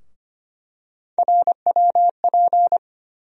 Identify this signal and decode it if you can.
Morse code, heard as a keyed tone, about 25 wpm, 700 Hz. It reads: RWP
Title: USA Call Sign Postfixes - 3 Letters 25wpm